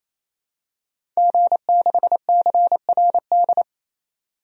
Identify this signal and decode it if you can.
Morse code, heard as a keyed tone, about 28 wpm, 700 Hz. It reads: G6CRD
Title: DXCC-100 Call Signs - 5 Characters - 28wpm